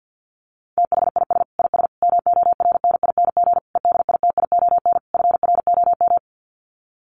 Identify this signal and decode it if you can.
Morse code, heard as a keyed tone, about 48 wpm, 700 Hz. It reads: THIS IS MORNING EDITION FROM
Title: Sets of 5 Words Encore2 50wpm